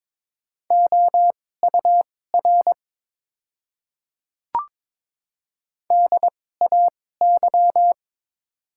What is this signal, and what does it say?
Morse code, heard as a keyed tone, about 22 wpm, spaced as for 16 wpm, 700 Hz. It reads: OUR DAY
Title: 2x Character Spacing - Top 300 Words - 22wpm